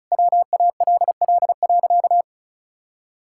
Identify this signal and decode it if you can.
Morse code, heard as a keyed tone, about 34 wpm, 700 Hz. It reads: WALL.
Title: Sentences from Top 1600 Words 35wpm